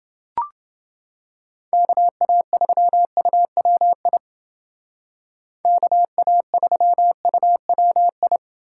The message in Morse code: KA3UWI KA3UWI

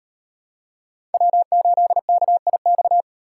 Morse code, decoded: W8KIX

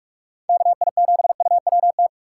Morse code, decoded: KI7UWT

